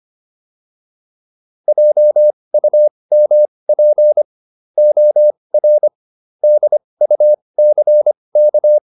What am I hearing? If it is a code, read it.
Morse code, heard as a keyed tone, about 25 wpm, spaced as for 20 wpm, 600 Hz. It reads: JUMP OR DUCK
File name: PR303 20